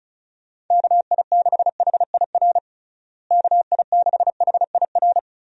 Morse code, heard as a keyed tone, about 34 wpm, 700 Hz. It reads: KI6HIR KI6HIR
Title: USA Call Signs - 6 Characters - Repeated 3x - 35wpm